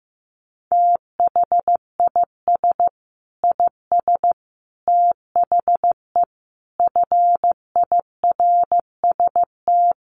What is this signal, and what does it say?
Morse code, heard as a keyed tone, about 15 wpm, 700 Hz. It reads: THIS IS THE FIRST
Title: Sets of 5 Words 15wpm